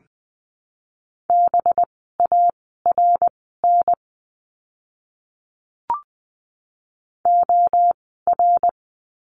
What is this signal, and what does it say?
Morse code, heard as a keyed tone, about 20 wpm, spaced as for 14 wpm, 700 Hz. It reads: BARN OR
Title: 185 Lesson 30f - Letter b - Words 20wpm